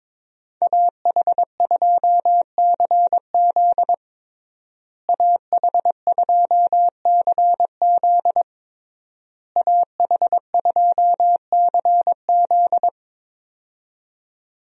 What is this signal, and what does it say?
Morse code, heard as a keyed tone, about 22 wpm, 700 Hz. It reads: AH2CZ AH2CZ AH2CZ
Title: USA Call Signs - 5 Characters - Repeated 3x - 22wpm